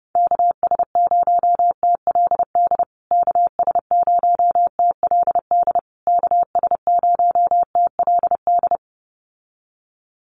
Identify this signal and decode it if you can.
Morse code, heard as a keyed tone, about 28 wpm, 700 Hz. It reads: KS0TLD KS0TLD KS0TLD
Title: USA Call Signs Repeated 3x 30wpm